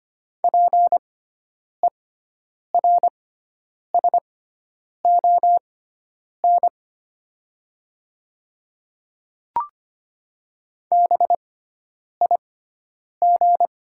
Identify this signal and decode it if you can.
Morse code, heard as a keyed tone, about 25 wpm, spaced as for 9 wpm, 700 Hz. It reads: PERSON BIG
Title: 6x Character Spacing - Top 500 Words - 25wpm